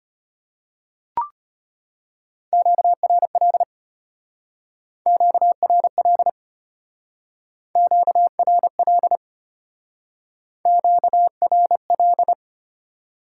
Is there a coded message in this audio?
Yes: Morse code from 2.5 to 12.3 s, heard as a keyed tone, about 33 wpm, 700 Hz. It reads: QRL QRL QRL QRL